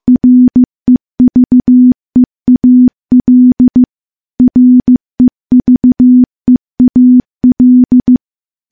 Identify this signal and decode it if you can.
Morse code, heard as a keyed tone, about 15 wpm, 250 Hz. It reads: REVEAL REVEAL
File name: morse / morse-training-15WPM